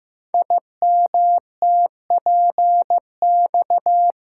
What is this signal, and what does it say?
Morse code, heard as a keyed tone, about 15 wpm, 700 Hz. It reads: IMTPX
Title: Code Groups - 5 Letters - 15wpm